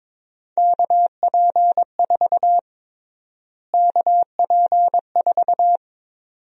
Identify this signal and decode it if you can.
Morse code, heard as a keyed tone, about 22 wpm, 700 Hz. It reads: KP4 KP4
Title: USA Call Sign Prefixes - Repeated 2x - 22wpm